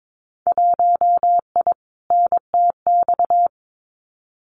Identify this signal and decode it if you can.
Morse code, heard as a keyed tone, about 22 wpm, 700 Hz. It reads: 1I NTX
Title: Winter Field Day Exchanges Encore 22wpm